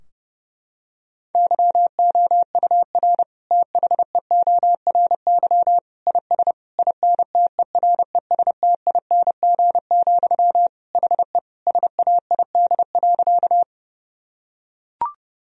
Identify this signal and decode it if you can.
Morse code, heard as a keyed tone, about 30 wpm, 700 Hz. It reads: YOUR THEORY IS INTERESTING, HE SAID.
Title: Sentences from Top 1000 Words - Review Part 3 - 30wpm